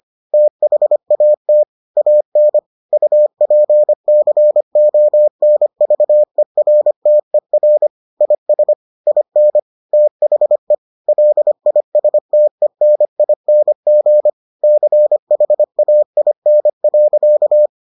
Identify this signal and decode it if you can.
Morse code, heard as a keyed tone, about 25 wpm, 600 Hz. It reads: THAT AN UPCONVERTER IS IN THE LISTENING CHAIN.